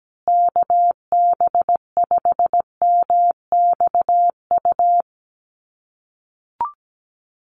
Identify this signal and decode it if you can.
Morse code, heard as a keyed tone, about 17 wpm, 700 Hz. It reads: KB5MXU
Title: USA Call Signs Repeated 2x 17wpm